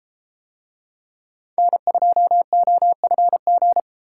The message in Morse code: N2OFG

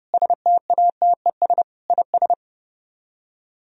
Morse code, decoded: STATES IS

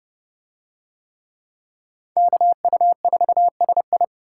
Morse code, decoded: KU4SI

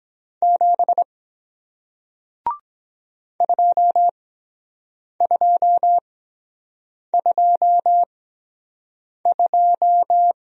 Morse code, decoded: 7 2222

